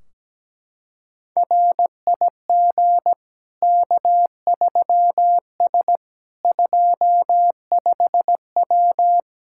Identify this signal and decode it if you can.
Morse code, heard as a keyed tone, about 17 wpm, 700 Hz. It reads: RIG K3S 25W
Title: K1USN SST QSO Practice 17wpm